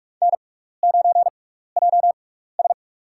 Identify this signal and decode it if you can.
Morse code, heard as a keyed tone, about 44 wpm, spaced as for 15 wpm, 700 Hz. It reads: N9JS
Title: USA Call Signs - 6x Character Spacing - 45wpm